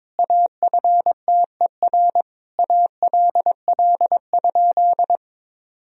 Morse code, heard as a keyed tone, about 22 wpm, 700 Hz. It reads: AFTER ALL?
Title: Sentences from Top 1000 Words - Review Part 2 - 22wpm